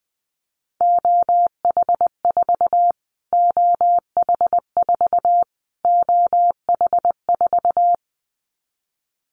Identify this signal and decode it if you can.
Morse code, heard as a keyed tone, about 20 wpm, 700 Hz. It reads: OH4 OH4 OH4
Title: Callsign Prefixes for Top 25 DXCC - Repeated 3x - 20wpm